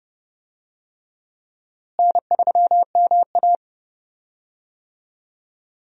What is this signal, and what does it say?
Morse code, heard as a keyed tone, about 31 wpm, 700 Hz. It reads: N3MA
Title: USA Call Signs - 4 Characters - Repeated 3x - 30wpm